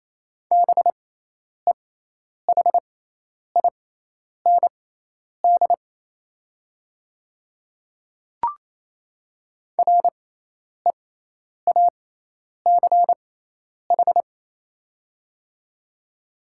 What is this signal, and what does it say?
Morse code, heard as a keyed tone, about 28 wpm, spaced as for 10 wpm, 700 Hz. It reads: BEHIND REACH
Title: 6x Character Spacing - Top 500 Words - 28wpm